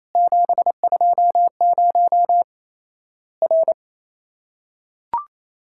Morse code, heard as a keyed tone, about 27 wpm, 700 Hz. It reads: 720 R
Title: Worked All Europe DX Contest - QTC Exchange - 28wpm